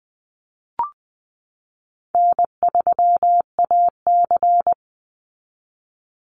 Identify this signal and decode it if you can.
Morse code, heard as a keyed tone, about 20 wpm, 700 Hz. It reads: N3AC
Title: Easy Call Signs Repeated 3x - 20wpm